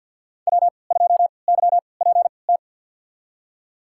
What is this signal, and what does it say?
Morse code, heard as a keyed tone, about 48 wpm, spaced as for 28 wpm, 700 Hz. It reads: W2YPT
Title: USA Call Signs - 5 Characters - 3x Character Spacing - 50wpm